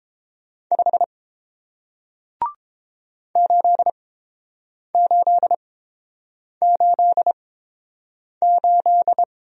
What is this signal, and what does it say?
Morse code, heard as a keyed tone, about 31 wpm, spaced as for 7 wpm, 700 Hz. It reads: 5 8888